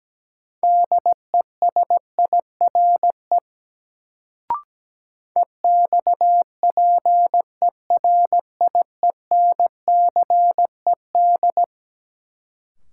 Morse code, heard as a keyed tone, about 17 wpm, 700 Hz. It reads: DESIRE EXPERIENCED
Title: Sentences from Top 1100 Words - New Word Drill - 17wpm